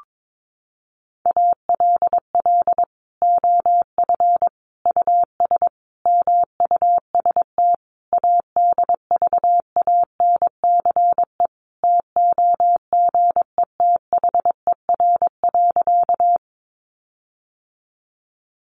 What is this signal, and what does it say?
Morse code, heard as a keyed tone, about 22 wpm, 700 Hz. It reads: ALL OF US MUST ADVANCE TOGETHER.
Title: Sentences from Top 1400 Words 22wpm